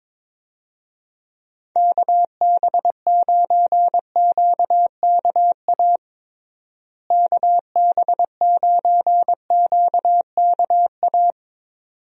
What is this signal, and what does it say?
Morse code, heard as a keyed tone, about 22 wpm, 700 Hz. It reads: KB9QKA KB9QKA
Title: USA Call Signs - 6 Characters - Repeated 2x - 22wpm